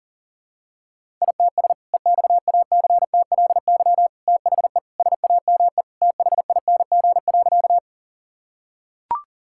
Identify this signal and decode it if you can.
Morse code, heard as a keyed tone, about 37 wpm, 700 Hz. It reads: ITS EXACTLY THE SAME THING.
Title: Sentences from Top 1000 Words - Review Part 6 - 40wpm